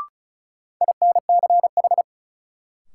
Morse code, heard as a keyed tone, about 36 wpm, 700 Hz. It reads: INCH
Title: Sentences from Top 1000 Words - Review Part 3 - New Word Drill - 35wpm